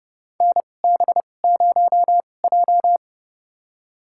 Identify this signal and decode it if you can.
Morse code, heard as a keyed tone, about 30 wpm, spaced as for 22 wpm, 700 Hz. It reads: NB0J
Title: USA Call Signs - 2x Character Spacing - 30wpm